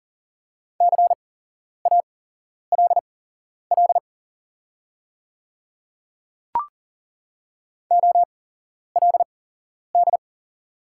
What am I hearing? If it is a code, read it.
Morse code, heard as a keyed tone, about 40 wpm, spaced as for 11 wpm, 700 Hz. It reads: CALL OLD
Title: 8x Character Spacing - Top 200 Words - 40wpm